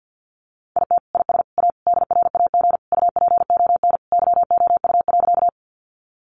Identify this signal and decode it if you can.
Morse code, heard as a keyed tone, about 50 wpm, 700 Hz. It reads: IT IS A DRAG UPON YOU.
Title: Sentences from Top 1500 Words 50wpm